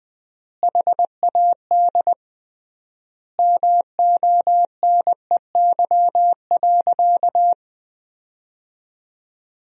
Morse code, heard as a keyed tone, about 20 wpm, 700 Hz. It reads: HAD MONEY.
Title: 3x Word-Spacing - Sentences from Top 500 Words - 20wpm